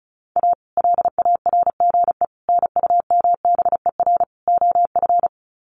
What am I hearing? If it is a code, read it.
Morse code, heard as a keyed tone, about 34 wpm, 700 Hz. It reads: A LARGE NUMBER OF
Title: Sets of 4 Words 35wpm